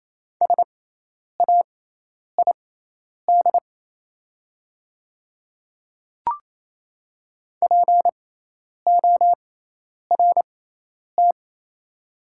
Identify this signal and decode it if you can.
Morse code, heard as a keyed tone, about 28 wpm, spaced as for 10 wpm, 700 Hz. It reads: SAID PORT
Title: 6x Character Spacing - Top 300 Words - 28wpm